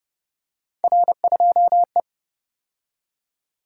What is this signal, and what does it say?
Morse code, heard as a keyed tone, about 29 wpm, 700 Hz. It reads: R2E